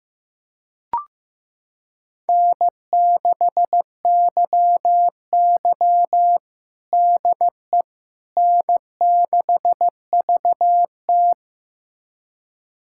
Morse code, heard as a keyed tone, about 16 wpm, 700 Hz. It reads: N6YY DE N6VT
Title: USA Callsign DE Callsign - 4 Characters - 15wpm